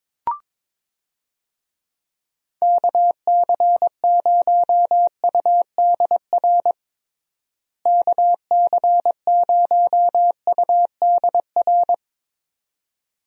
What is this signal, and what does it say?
Morse code, heard as a keyed tone, about 22 wpm, 700 Hz. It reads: KC0UDR KC0UDR
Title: USA Call Signs - 6 Characters - Repeated 2x - 22wpm